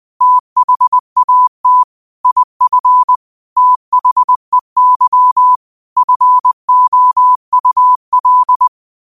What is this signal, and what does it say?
Morse code, heard as a keyed tone, about 20 wpm, 1000 Hz. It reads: THAT IF THEY FOUL
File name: qotdfiles / 20wpm